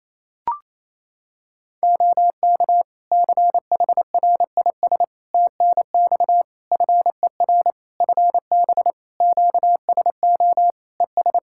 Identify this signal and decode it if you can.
Morse code, heard as a keyed tone, about 28 wpm, 700 Hz. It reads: OK CHRIS TNX FER FB QSO ES
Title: Practice QSO - Exchange 4 - Ending - 28wpm